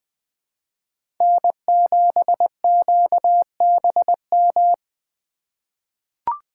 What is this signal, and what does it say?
Morse code, heard as a keyed tone, about 20 wpm, 700 Hz. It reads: N7QBM
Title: USA Call Signs - 5 Characters - 2x Character Spacing - 20wpm